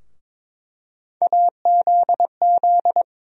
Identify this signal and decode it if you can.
Morse code, heard as a keyed tone, about 22 wpm, 700 Hz. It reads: AZZ
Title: USA Call Sign Postfixes - 3 Letters 22wpm